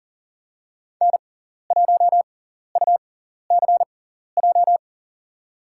Morse code, heard as a keyed tone, about 37 wpm, spaced as for 13 wpm, 700 Hz. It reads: N1UCJ
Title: USA Call Signs - 5 Characters - 6x Character Spacing - 40wpm